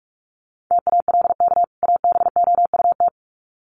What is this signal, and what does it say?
Morse code, heard as a keyed tone, about 44 wpm, 700 Hz. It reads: TALK ABOUT
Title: Sets of 2 Words 45wpm